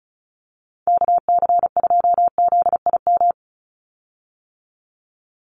Morse code, heard as a keyed tone, about 34 wpm, 700 Hz. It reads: KC2ZIM
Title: USA Call Signs 35wpm